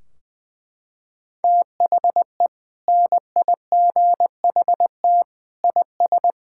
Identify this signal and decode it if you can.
Morse code, heard as a keyed tone, about 20 wpm, 700 Hz. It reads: THE NIGHT IS
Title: Sentences from Top 1600 Words 20wpm